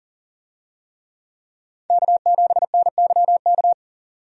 Morse code, heard as a keyed tone, about 40 wpm, 700 Hz. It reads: K7NYK
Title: USA Call Signs - 5 Characters - 40wpm